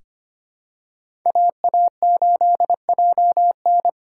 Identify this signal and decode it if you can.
Morse code, heard as a keyed tone, about 25 wpm, 700 Hz. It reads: AA8JN
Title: USA Call Signs - 5 Characters - 4x Character Spacing - 25wpm